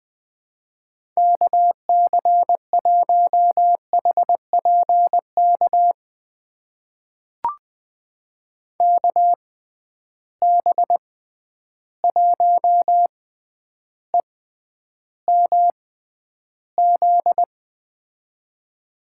Morse code, heard as a keyed tone, about 20 wpm, spaced as for 10 wpm, 700 Hz. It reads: KC1HPK KB1EMZ
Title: USA Call Signs - 6 Characters - 6x Character Spacing - 20wpm